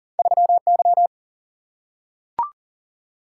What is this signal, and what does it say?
Morse code, heard as a keyed tone, about 38 wpm, 700 Hz. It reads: 3Y
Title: Single-Letter Single-Number 40wpm